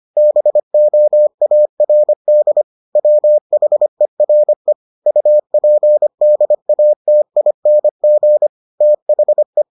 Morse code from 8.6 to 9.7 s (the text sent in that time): THE